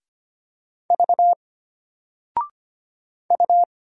Morse code, heard as a keyed tone, about 25 wpm, 700 Hz. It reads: V U
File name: Single-Letter-Number-25wpm